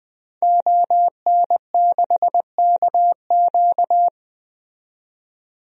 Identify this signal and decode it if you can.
Morse code, heard as a keyed tone, about 20 wpm, 700 Hz. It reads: ON6KQ
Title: DXCC-100 Call Signs - 5 Characters - 20wpm